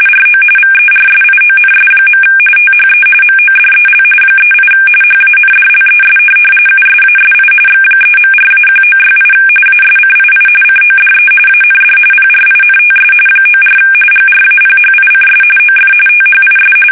RTTY, about 45.45 baud. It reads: RO VERSIONS. THE MODEL 38 HANDLES USASI X3.4-1968 ASCII CODE AND PRINTS 94 GRAPHICS OF THIS
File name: fsk850 / teletype-corp